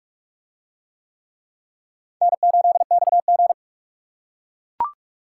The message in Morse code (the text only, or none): N8XG